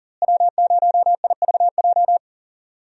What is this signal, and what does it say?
Morse code, heard as a keyed tone, about 40 wpm, 700 Hz. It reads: W0IVJ
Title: USA Call Signs - 5 Characters - Repeated 2x - 40wpm